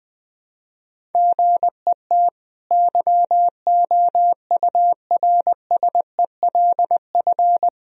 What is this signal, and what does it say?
Morse code, heard as a keyed tone, about 20 wpm, 700 Hz. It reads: GET YOURSELF
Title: Sentences from Top 1000 Words - Review Part 1 - 20wpm